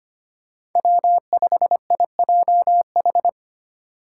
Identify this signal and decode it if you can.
Morse code, heard as a keyed tone, about 25 wpm, 700 Hz. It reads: W5IJH
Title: K1USN SST QSO Practice - 25wpm